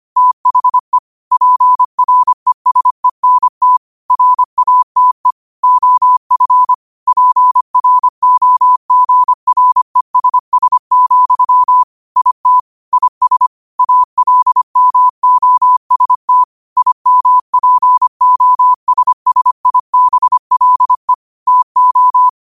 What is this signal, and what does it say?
Morse code, heard as a keyed tone, about 25 wpm, 1000 Hz. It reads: THE PRESENT RATE OF PROGRESS, IT IS ALMOST IMPOSSIBLE TO
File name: qotdfiles / 25wpm